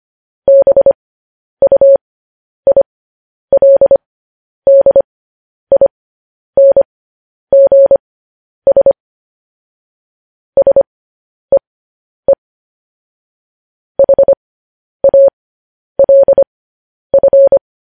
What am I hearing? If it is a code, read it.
Morse code, heard as a keyed tone, about 25 wpm, spaced as for 10 wpm, 550 Hz. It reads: BUILDINGS SEE HALF